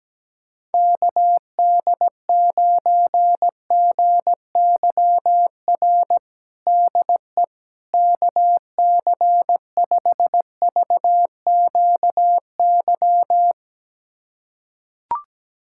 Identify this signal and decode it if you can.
Morse code, heard as a keyed tone, about 17 wpm, 700 Hz. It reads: KD9GYR DE KC5VQY
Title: USA Callsign DE Callsign - 6 Characters - 17wpm